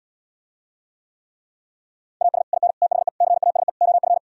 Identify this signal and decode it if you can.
Morse code, heard as a keyed tone, about 77 wpm, 700 Hz. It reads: AS IN THE CASE OF